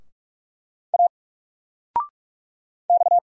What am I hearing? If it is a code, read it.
Morse code, heard as a keyed tone, about 46 wpm, 700 Hz. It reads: A X
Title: Single Letter-Number 45wpm